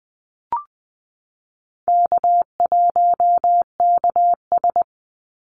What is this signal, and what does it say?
Morse code, heard as a keyed tone, about 19 wpm, 700 Hz. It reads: K1KS
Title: Easy Call Signs Repeated 2x - 20wpm